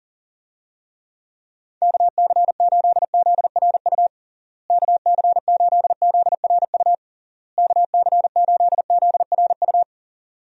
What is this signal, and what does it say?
Morse code, heard as a keyed tone, about 40 wpm, 700 Hz. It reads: KC8ZRU KC8ZRU KC8ZRU
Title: USA Call Signs - 6 Characters - Repeated 3x - 40wpm